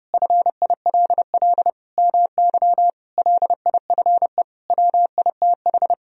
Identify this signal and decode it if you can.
Morse code, heard as a keyed tone, about 32 wpm, 700 Hz. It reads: FILL MY LIFE WITH
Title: Sentences from Top 1000 Words - Review Part 3 - 30wpm